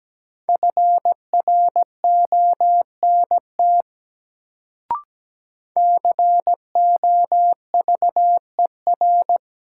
Morse code, heard as a keyed tone, about 17 wpm, 700 Hz. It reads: FRONT COVER
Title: Sentences from Top 500 Words - New Word Drill - 17wpm